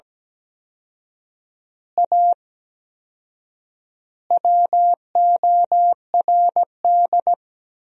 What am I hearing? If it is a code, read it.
Morse code, heard as a keyed tone, about 17 wpm, 700 Hz. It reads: A WORD
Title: 4x Word-Spacing - Sentences from Top 300 Words - 17wpm